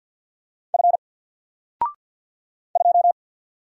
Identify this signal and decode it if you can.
Morse code, heard as a keyed tone, about 50 wpm, 700 Hz. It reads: F 2
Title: Single Letter-Number 50wpm